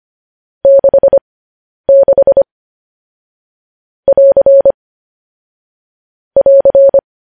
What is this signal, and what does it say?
Morse code, heard as a keyed tone, about 25 wpm, spaced as for 10 wpm, 550 Hz. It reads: 66 <AR> <AR>
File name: podcast / News headlines-1x-25x10